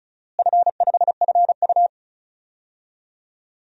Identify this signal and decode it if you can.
Morse code, heard as a keyed tone, about 36 wpm, 700 Hz. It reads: F5FU
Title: DXCC-100 Call Signs - 4 Characters - Repeated 2x - 35wpm